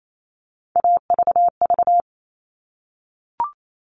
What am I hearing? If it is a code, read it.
Morse code, heard as a keyed tone, about 28 wpm, 700 Hz. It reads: AVV
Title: All Cut Numbers - 3-Digit 28wpm